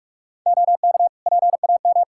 Morse code, heard as a keyed tone, about 41 wpm, 700 Hz. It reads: OK PAM
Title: K1USN SST QSO Practice 45wpm